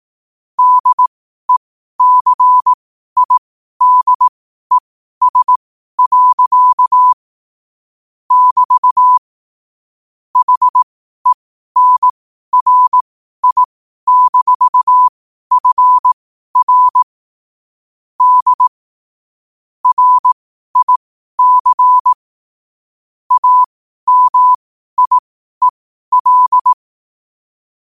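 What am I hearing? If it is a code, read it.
Morse code, heard as a keyed tone, about 18 wpm, spaced as for 12 wpm, 1000 Hz. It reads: DECIDES. <BT> HENRI-FR D RIC AMIEL
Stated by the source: Morse code quote practice at 12 words per minute.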